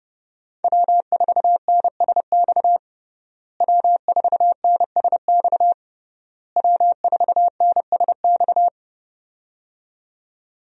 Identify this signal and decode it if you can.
Morse code, heard as a keyed tone, about 30 wpm, 700 Hz. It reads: W4NSX W4NSX W4NSX
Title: USA Call Signs - 5 Characters - Repeated 3x - 30wpm